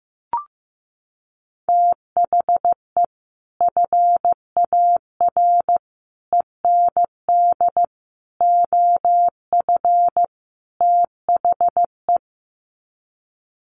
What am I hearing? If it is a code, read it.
Morse code, heard as a keyed tone, about 15 wpm, 700 Hz. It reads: THE FAR END OF THE
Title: Sets of 5 Words 15wpm